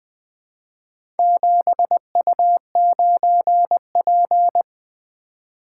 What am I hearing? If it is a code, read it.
Morse code, heard as a keyed tone, about 20 wpm, 700 Hz. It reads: 7U9P
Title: Four Letter-Number Combinations - 20wpm